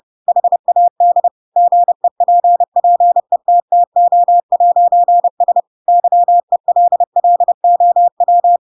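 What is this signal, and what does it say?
Morse code, heard as a keyed tone, about 30 wpm, 700 Hz. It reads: HAD GEPPETTO'S YELLOW
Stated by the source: Chapter 2 of The Adventures of Pinocchio read as a book in Morse code (CW) at 30 wpm, by ik1hge.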